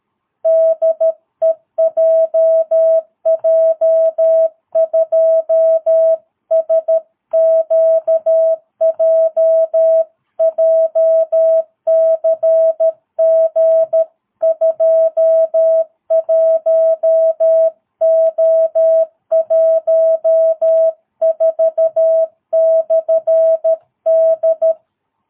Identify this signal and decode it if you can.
Morse code, heard as a keyed tone, about 13 wpm, 660 Hz. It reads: DE JJ2SQJ JCG21O14/D
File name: jh8cbh / 2022cw